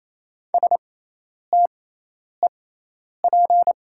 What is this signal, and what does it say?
Morse code, heard as a keyed tone, about 28 wpm, spaced as for 10 wpm, 700 Hz. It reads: STEP
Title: 6x Character Spacing - Top 500 Words - 28wpm